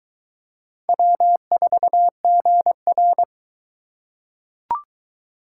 Morse code, heard as a keyed tone, about 22 wpm, 700 Hz. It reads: W4GR